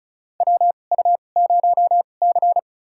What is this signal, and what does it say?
Morse code, heard as a keyed tone, about 36 wpm, spaced as for 26 wpm, 690 Hz. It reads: WU0C
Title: USA Call Signs - 4 Characters - 2x Character Spacing - 35wpm